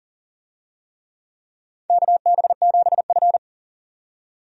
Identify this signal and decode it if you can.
Morse code, heard as a keyed tone, about 40 wpm, 700 Hz. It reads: KB7F
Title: USA Call Signs - 4 Characters - 40wpm